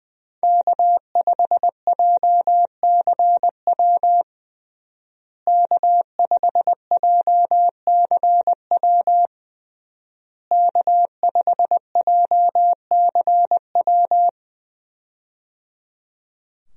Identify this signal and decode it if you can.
Morse code, heard as a keyed tone, about 20 wpm, 700 Hz. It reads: K5JCW K5JCW K5JCW
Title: USA Call Signs - 5 Characters - Repeated 3x - 20wpm